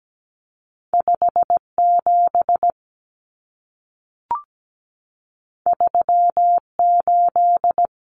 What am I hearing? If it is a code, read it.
Morse code, heard as a keyed tone, about 17 wpm, 700 Hz. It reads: 57 38
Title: Common Cut Numbers - 2-Digit 17wpm